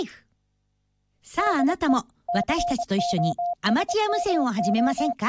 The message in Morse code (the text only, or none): DE COSMO